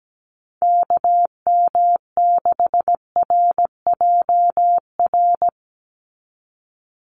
Morse code, heard as a keyed tone, about 17 wpm, 700 Hz. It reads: KM6RJR
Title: USA Call Signs 17wpm